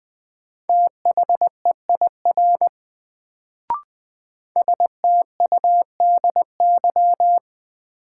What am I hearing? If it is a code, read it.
Morse code, heard as a keyed tone, about 21 wpm, 700 Hz. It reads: THEIR STUDY
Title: Sentences from Top 300 Words - New Word Drill - 20wpm